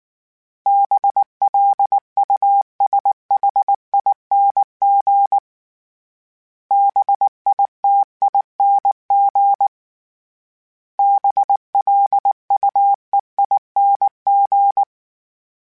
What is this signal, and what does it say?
Morse code, heard as a keyed tone, about 19 wpm, 800 Hz. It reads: BLUSHING BITING BLUEING
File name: Special INGs / Special INGs-mp3s / Sngl Ing @18